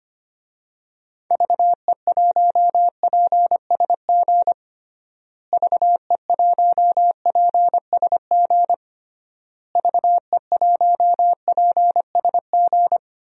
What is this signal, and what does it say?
Morse code, heard as a keyed tone, about 25 wpm, 700 Hz. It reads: VE1PSG VE1PSG VE1PSG
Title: DXCC-100 Call Signs - 6 Characters - Repeated 3x - 25wpm